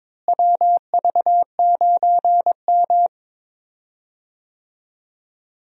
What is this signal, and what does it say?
Morse code, heard as a keyed tone, about 21 wpm, 700 Hz. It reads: WV9M
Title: USA Call Signs - 4 Characters - Repeated 2x - 22wpm